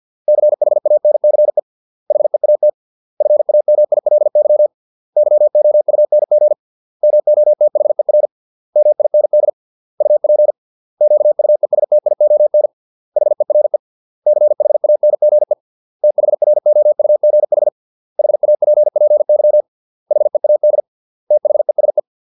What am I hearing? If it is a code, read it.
Morse code, heard as a keyed tone, about 50 wpm, 600 Hz. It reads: CHANCE HEAT FAMILY YOUNG MOTHER MIND UP QUESTION HERE CHANGE THROUGH HAPPY HEAD THESE